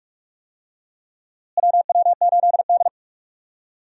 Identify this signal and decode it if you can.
Morse code, heard as a keyed tone, about 42 wpm, 700 Hz. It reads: WW8D
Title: USA Call Signs - 4 Characters - Repeated 2x - 45wpm